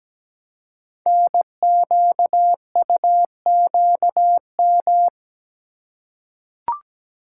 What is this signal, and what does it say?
Morse code, heard as a keyed tone, about 17 wpm, 700 Hz. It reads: NQUQM
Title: Code Groups - 5 Letters - 17wpm